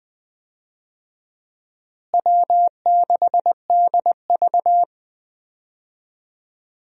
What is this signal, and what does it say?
Morse code, heard as a keyed tone, about 21 wpm, 700 Hz. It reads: W6DV
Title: USA Call Signs - 4 Characters - Repeated 2x - 20wpm